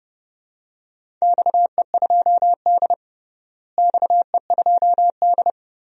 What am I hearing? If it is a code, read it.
Morse code, heard as a keyed tone, about 30 wpm, 700 Hz. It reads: XE2D XE2D
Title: DXCC-100 Call Signs - 4 Characters - Repeated 3x - 30wpm